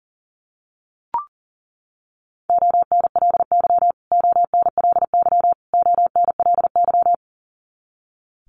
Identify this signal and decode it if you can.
Morse code, heard as a keyed tone, about 40 wpm, 700 Hz. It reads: ONLY ONLY ONLY
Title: Top 500 Words Repeated 40wpm